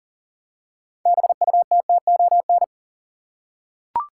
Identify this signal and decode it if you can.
Morse code, heard as a keyed tone, about 40 wpm, 700 Hz. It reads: BUTTON
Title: Six Letter Words - 40wpm